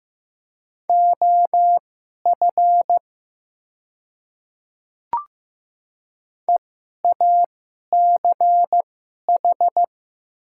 Morse code, heard as a keyed tone, about 15 wpm, spaced as for 11 wpm, 700 Hz. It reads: OF EACH
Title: 2x Character Spacing - Top 100 Words - 15wpm